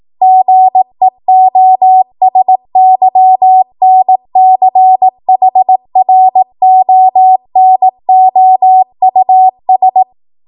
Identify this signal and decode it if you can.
Morse code, heard as a keyed tone, about 18 wpm, 750 Hz. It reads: GEOSYNCHRONOUS
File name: Morse / 18WPM